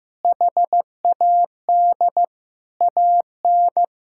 Morse code, heard as a keyed tone, about 14 wpm, 700 Hz. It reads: HAD AN
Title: Sentences from Top 1000 Words - Review Part 6 - 15wpm